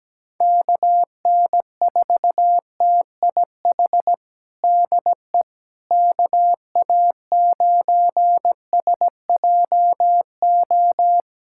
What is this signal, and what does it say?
Morse code, heard as a keyed tone, about 17 wpm, 700 Hz. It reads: KN4TIH DE KA9SJO
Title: USA Callsign DE Callsign - 6 Characters - 17wpm